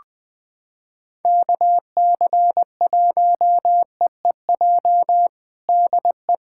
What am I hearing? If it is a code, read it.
Morse code, heard as a keyed tone, about 20 wpm, 700 Hz. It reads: KC1EEJ DE
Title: USA Callsign DE Callsign - 6 Characters - 20wpm